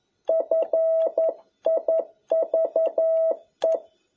Morse code, heard as a keyed tone, about 11 wpm, 650 Hz. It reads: FIVE
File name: jh8cbh / 2021cw